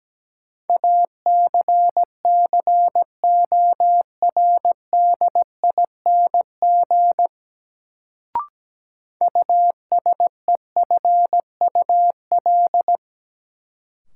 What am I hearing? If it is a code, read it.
Morse code, heard as a keyed tone, about 17 wpm, 700 Hz. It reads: ACCORDING USEFUL
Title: Sentences from Top 700 Words - New Word Drill - 17wpm